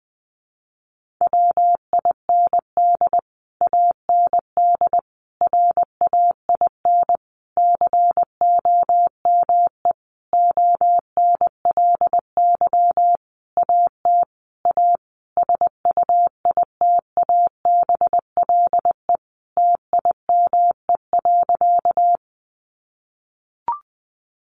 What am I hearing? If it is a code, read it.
Morse code, heard as a keyed tone, about 20 wpm, 700 Hz. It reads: WIND AND RAIN COME ONLY AT A SUITABLE TIME.
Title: Sentences from Top 1400 Words 20wpm